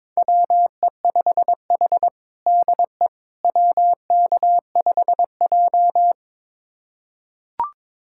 Morse code, heard as a keyed tone, about 22 wpm, 700 Hz. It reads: WE5H DE WK5J
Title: USA Callsign DE Callsign - 4 Characters - 22wpm